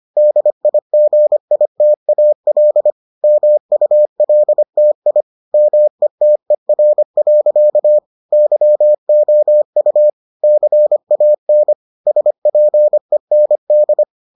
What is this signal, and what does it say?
Morse code, heard as a keyed tone, about 25 wpm, 600 Hz. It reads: DIGITAL MULTI METER. YOU CAN SPEND